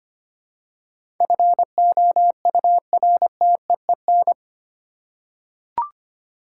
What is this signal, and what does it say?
Morse code, heard as a keyed tone, about 25 wpm, 700 Hz. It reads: FOURTEEN
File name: 8 Letter Words - First 3 Letters Given - 25wpm